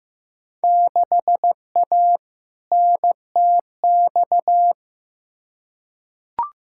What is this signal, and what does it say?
Morse code, heard as a keyed tone, about 15 wpm, 700 Hz. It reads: 6A NTX
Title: Field Day Exchanges - 15wpm